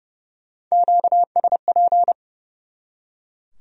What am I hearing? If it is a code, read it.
Morse code, heard as a keyed tone, about 37 wpm, 700 Hz. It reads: QSP
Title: Speed-Racing Q-Codes - 30wpm